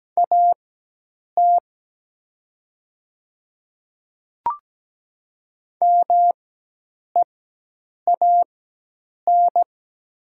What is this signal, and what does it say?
Morse code, heard as a keyed tone, about 17 wpm, spaced as for 8 wpm, 700 Hz. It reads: AT MEAN
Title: 4x Character Spacing - Top 500 Words - 17wpm